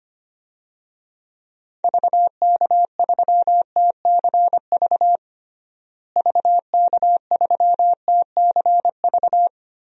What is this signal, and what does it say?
Morse code, heard as a keyed tone, about 25 wpm, 700 Hz. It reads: VK3TCV VK3TCV
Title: DXCC-100 Call Signs - 6 Characters - Repeated 2x - 25wpm